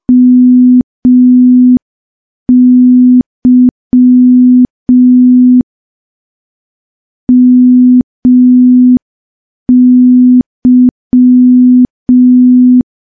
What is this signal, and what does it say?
Morse code, heard as a keyed tone, about 5 wpm, 250 Hz. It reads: MY MY
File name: morse / morse-training-05WPM